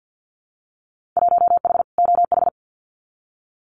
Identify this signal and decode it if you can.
Morse code, heard as a keyed tone, about 50 wpm, 700 Hz. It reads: 1H OH
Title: Winter Field Day Exchanges 50wpm